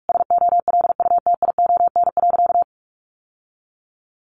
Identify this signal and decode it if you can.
Morse code, heard as a keyed tone, about 44 wpm, 700 Hz. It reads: SOLUTION.
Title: Sentences from Top 1300 Words 45wpm